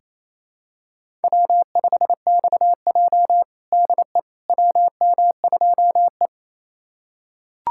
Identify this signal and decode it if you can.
Morse code, heard as a keyed tone, about 28 wpm, 700 Hz. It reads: W5XJ DE WM2E
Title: USA Callsign DE Callsign - 4 Characters - 28wpm